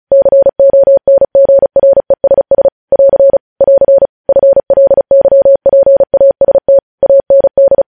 Morse code, heard as a keyed tone, about 36 wpm, 550 Hz. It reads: CONGRESS <AR> <AR> FLYPAST AND
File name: podcast / News headlines-1x-35x35